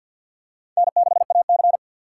Morse code, heard as a keyed tone, about 47 wpm, 700 Hz. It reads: N6AC
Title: USA Call Signs - 4 Characters - 2x Character Spacing - 50wpm